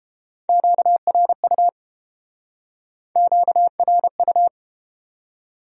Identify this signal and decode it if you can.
Morse code, heard as a keyed tone, about 33 wpm, 700 Hz. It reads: QRU QRU